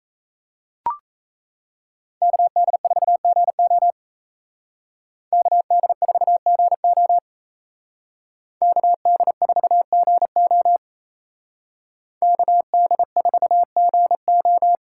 Morse code, heard as a keyed tone, about 34 wpm, 700 Hz. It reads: KD4GO KD4GO KD4GO KD4GO
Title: Speed Racing - USA 2x2 Call Signs 28wpm